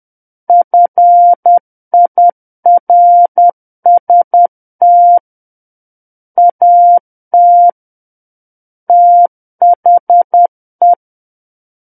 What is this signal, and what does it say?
Morse code, heard as a keyed tone, about 10 wpm, 700 Hz. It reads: FIRST AT THE
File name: Book One / 01-Chapter 1 - 10 WPM